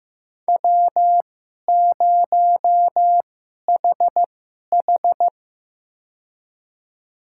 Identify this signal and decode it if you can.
Morse code, heard as a keyed tone, about 15 wpm, spaced as for 11 wpm, 700 Hz. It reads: W0HH
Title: USA Call Signs - 4 Characters - 2x Character Spacing - 15wpm